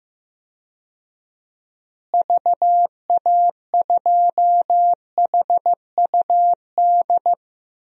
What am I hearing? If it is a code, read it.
Morse code, heard as a keyed tone, about 15 wpm, 700 Hz. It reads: VA2HUD
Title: DXCC-100 Call Signs - 6 Characters - Repeated 2x - 15wpm